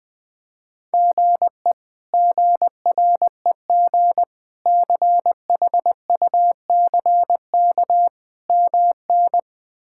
Morse code, heard as a keyed tone, about 20 wpm, 700 Hz. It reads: GE GREG CHUCK MN
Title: K1USN SST QSO Practice - 20wpm